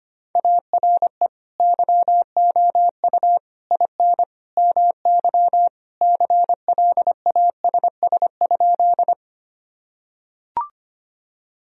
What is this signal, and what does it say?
Morse code, heard as a keyed tone, about 25 wpm, 700 Hz. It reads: ARE YOU IN MY CLASS?
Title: Sentences from Top 1000 Words - Review Part 2 - 25wpm